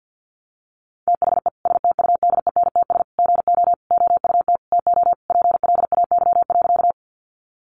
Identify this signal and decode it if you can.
Morse code, heard as a keyed tone, about 50 wpm, 700 Hz. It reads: THE STUDENTS GO OUT TO PLAY.
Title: Easy Phrases 50wpm